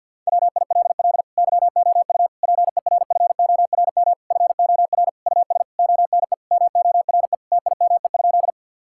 Morse code, heard as a keyed tone, about 50 wpm, 700 Hz. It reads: WILL YOU PERFORM FOR US ONE MORE TIME?
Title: Sentences from Top 1000 Words - Review Part 5 - 50wpm